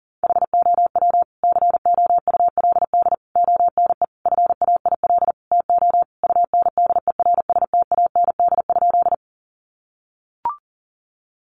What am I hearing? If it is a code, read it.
Morse code, heard as a keyed tone, about 40 wpm, 700 Hz. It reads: HOW COULD ONE FAIL TO UNDERSTAND?
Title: Sentences from Top 1100 Words 40wpm